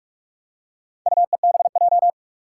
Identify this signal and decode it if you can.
Morse code, heard as a keyed tone, about 44 wpm, 700 Hz. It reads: UEBJ
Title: Four Challenging Letter-Number Combinations 45wpm